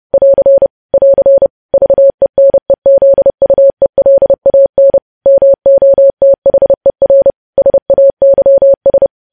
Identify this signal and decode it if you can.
Morse code, heard as a keyed tone, about 30 wpm, 550 Hz. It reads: <AR> <AR> VENEZUELAN MOTHER SAYS
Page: news headlines in morse code at 30 wpm